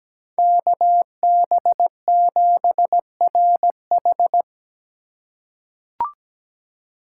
Morse code, heard as a keyed tone, about 17 wpm, 700 Hz. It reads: KB7RH
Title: USA Call Signs - 5 Characters - 2x Character Spacing - 17wpm